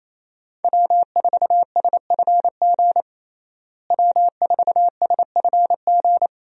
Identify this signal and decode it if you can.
Morse code, heard as a keyed tone, about 28 wpm, 700 Hz. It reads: W4SFG W4SFG
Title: USA Call Signs - 5 Characters - Repeated 2x - 28wpm